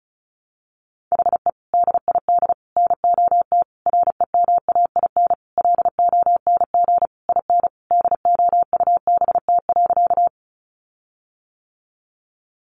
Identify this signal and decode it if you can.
Morse code, heard as a keyed tone, about 34 wpm, 700 Hz. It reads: HE DID NOT REMAIN LONG IN DOUBT.
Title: Sentences from Top 1500 Words 35wpm